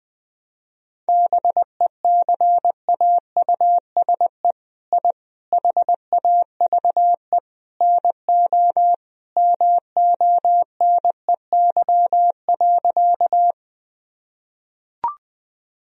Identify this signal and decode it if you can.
Morse code, heard as a keyed tone, about 21 wpm, 700 Hz. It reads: BECAUSE I HAVE NO MONEY.
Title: Sentences from Top 1000 Words - Review Part 3 - 20wpm